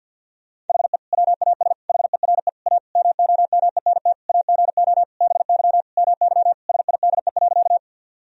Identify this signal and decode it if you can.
Morse code, heard as a keyed tone, about 52 wpm, 700 Hz. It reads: HE WAS HERE A MOMENT AGO BY MY SIDE.
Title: Sentences from Top 1000 Words - Review Part 3 - 50wpm